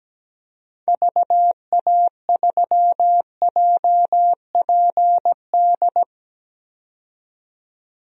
Morse code, heard as a keyed tone, about 17 wpm, 700 Hz. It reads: VA3JPD
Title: DXCC-100 Call Signs - 6 Characters - 17wpm